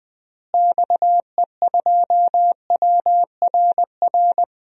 Morse code, heard as a keyed tone, about 20 wpm, 700 Hz. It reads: XE2WRR
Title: DXCC-100 Call Signs - 6 Characters - Repeated 2x - 20wpm